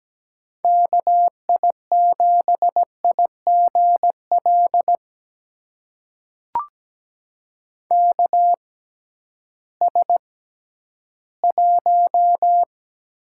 Morse code, heard as a keyed tone, about 17 wpm, spaced as for 11 wpm, 700 Hz. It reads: KI7IGL KS1
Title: USA Call Signs - 6x Character Spacing - 17wpm